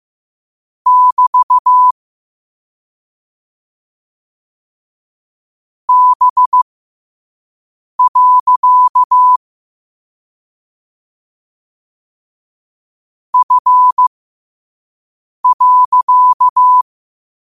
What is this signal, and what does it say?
Morse code, heard as a keyed tone, about 15 wpm, spaced as for 5 wpm, 1000 Hz. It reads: <BT> B. F.
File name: qotdfiles / 5wpm